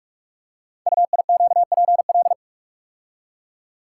Morse code, heard as a keyed tone, about 44 wpm, 700 Hz. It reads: UIQPL
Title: Code Groups - 5 Letters - 45wpm